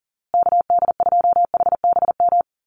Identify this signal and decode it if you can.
Morse code, heard as a keyed tone, about 40 wpm, 700 Hz. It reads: KD2HBM
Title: USA Call Signs Repeated 3x 40wpm